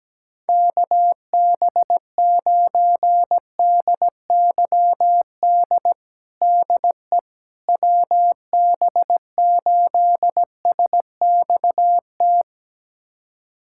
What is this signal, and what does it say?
Morse code, heard as a keyed tone, about 17 wpm, 700 Hz. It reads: KB9DYD DE WB8SXT
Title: USA Callsign DE Callsign - 6 Characters - 17wpm